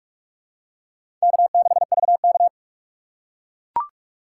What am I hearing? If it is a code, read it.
Morse code, heard as a keyed tone, about 41 wpm, 700 Hz. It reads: K6VK
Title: USA Call Signs - 4 Characters - 6x Character Spacing - 45wpm